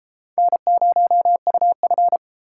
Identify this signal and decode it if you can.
Morse code, heard as a keyed tone, about 33 wpm, 700 Hz. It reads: N0UF